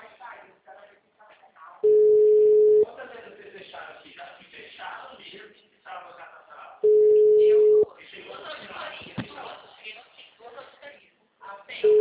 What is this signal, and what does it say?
Ringback tone (Europe, 425 Hz), with speech.